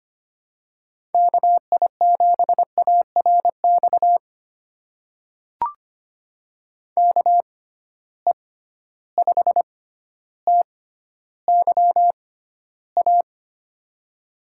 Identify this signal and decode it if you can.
Morse code, heard as a keyed tone, about 25 wpm, spaced as for 13 wpm, 700 Hz. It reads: KI7ARX KE5TYA
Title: USA Call Signs - 6 Characters - 6x Character Spacing - 25wpm